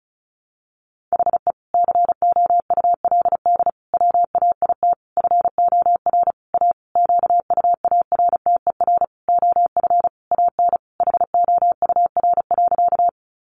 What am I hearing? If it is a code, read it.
Morse code, heard as a keyed tone, about 34 wpm, 700 Hz. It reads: HE COULD WAIT FOR A QUARTER OF AN HOUR.
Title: Sentences from Top 1300 Words 35wpm